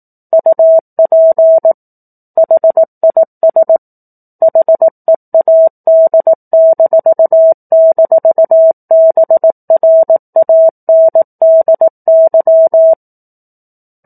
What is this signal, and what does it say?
Morse code, heard as a keyed tone, about 18 wpm, 650 Hz. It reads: UP HIS HEAD--BRANDY
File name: 04-Chapter-04 18 WPM